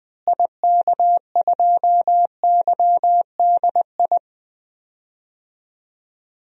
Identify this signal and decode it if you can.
Morse code, heard as a keyed tone, about 20 wpm, 700 Hz. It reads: IK2YDI
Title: DXCC-100 Call Signs - 6 Characters - Repeated 3x - 20wpm